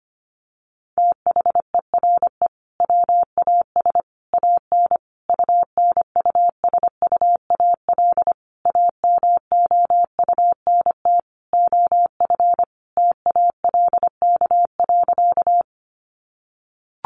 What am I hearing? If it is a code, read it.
Morse code, heard as a keyed tone, about 25 wpm, 700 Hz. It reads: THERE WAS AN UNUSUAL AMOUNT OF TALK.
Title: Sentences from Top 1400 Words 25wpm